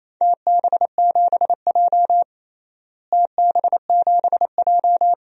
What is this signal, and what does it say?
Morse code, heard as a keyed tone, about 28 wpm, 700 Hz. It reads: TB7J TB7J
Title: DXCC-100 Call Signs - 4 Characters - Repeated 2x - 28wpm